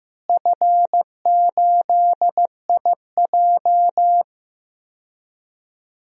Morse code, heard as a keyed tone, about 15 wpm, 700 Hz. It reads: F8IJ
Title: DXCC-100 Call Signs - 4 Characters - Repeated 3x - 15wpm